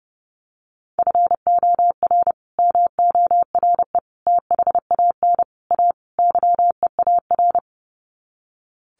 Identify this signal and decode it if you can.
Morse code, heard as a keyed tone, about 30 wpm, 700 Hz. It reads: FOR MORE THAN A YEAR
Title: Sets of 5 Words 30wpm